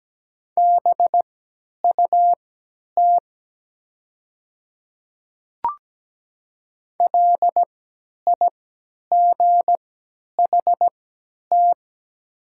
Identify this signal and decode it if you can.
Morse code, heard as a keyed tone, about 17 wpm, spaced as for 10 wpm, 700 Hz. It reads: BUT LIGHT
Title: 3x Character Spacing - Top 200 Words - 17wpm